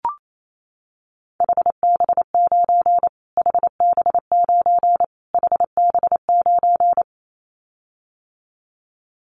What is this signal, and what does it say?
Morse code, heard as a keyed tone, about 28 wpm, 700 Hz. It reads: HB9 HB9 HB9
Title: Callsign Prefixes for Top 25 DXCC - Repeated 3x - 28wpm